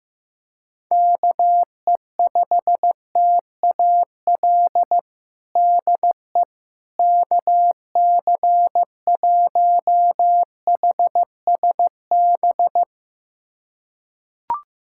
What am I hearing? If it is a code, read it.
Morse code, heard as a keyed tone, about 15 wpm, 700 Hz. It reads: KE5TAL DE KC1HSB
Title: USA Callsign DE Callsign - 6 Characters - 15wpm